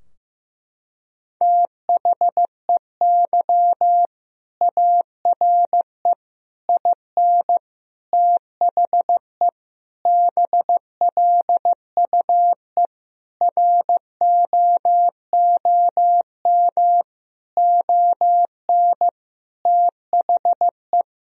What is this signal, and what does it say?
Morse code, heard as a keyed tone, about 15 wpm, 700 Hz. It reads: THEY ARE IN THE BLUE ROOM ON THE
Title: Sentences from Top 1000 Words - Review Part 3 - 15wpm